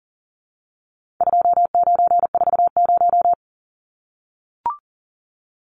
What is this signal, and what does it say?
Morse code, heard as a keyed tone, about 40 wpm, 700 Hz. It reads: 2940
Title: Numbers - 4-Digit 40wpm